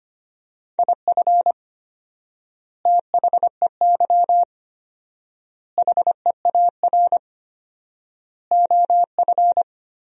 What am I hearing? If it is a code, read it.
Morse code, heard as a keyed tone, about 25 wpm, 700 Hz. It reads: IF THEY HEAR OF
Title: 4x Word-Spacing - Sentences from Top 400 Words - 25wpm